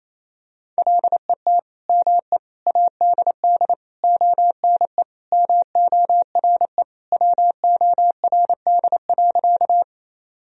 Morse code, heard as a keyed tone, about 28 wpm, 700 Hz. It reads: LET ME ADD ONE MORE WORD.
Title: Sentences from Top 1000 Words - Review Part 1 - 28wpm